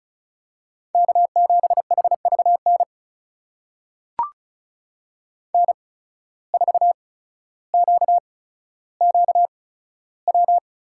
Morse code, heard as a keyed tone, about 34 wpm, spaced as for 15 wpm, 700 Hz. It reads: K7HVN N4QQW
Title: USA Call Signs - 5 Characters - 8x Character Spacing - 35wpm